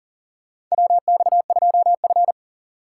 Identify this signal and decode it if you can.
Morse code, heard as a keyed tone, about 37 wpm, 700 Hz. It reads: WX2F
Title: USA Call Signs - 4 Characters - 6x Character Spacing - 40wpm